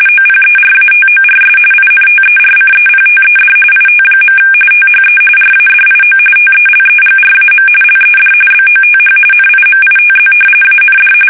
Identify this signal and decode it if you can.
RTTY, about 45.45 baud. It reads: OPERATORS INTERCEPTED IT. THE RADIO TELEGRAM LED BOTH NIMITZ AND
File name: fsk850 / battleshipnj